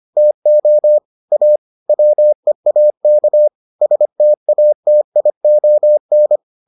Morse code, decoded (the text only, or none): TO A WEAK STATION